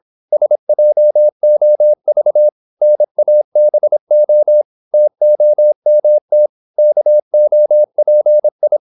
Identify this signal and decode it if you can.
Morse code, heard as a keyed tone, about 26 wpm, 600 Hz. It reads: SJOV NABO TOMT KOPI
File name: Korte ord DK 26wpm